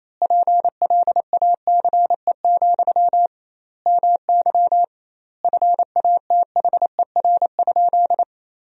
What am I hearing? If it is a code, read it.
Morse code, heard as a keyed tone, about 28 wpm, 700 Hz. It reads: PLACE, MY FATHER?
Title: 2x Word-Spacing - Sentences from Top 400 Words - 28wpm